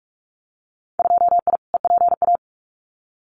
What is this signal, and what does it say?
Morse code, heard as a keyed tone, about 44 wpm, 700 Hz. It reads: 2I EPA
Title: Winter Field Day Exchanges 45wpm